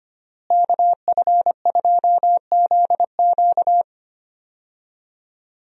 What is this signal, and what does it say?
Morse code, heard as a keyed tone, about 25 wpm, 700 Hz. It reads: KF2ZQ
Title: USA Call Signs - 5 Characters - Repeated 3x - 25wpm